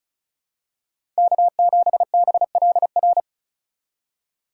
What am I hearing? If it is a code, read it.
Morse code, heard as a keyed tone, about 36 wpm, 700 Hz. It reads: K7BLR
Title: USA Call Signs - 5 Characters - 4x Character Spacing - 35wpm